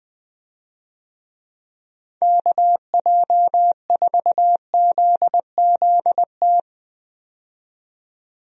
Morse code, heard as a keyed tone, about 21 wpm, 700 Hz. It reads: KJ4ZZT
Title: USA Call Signs - 6 Characters - 20wpm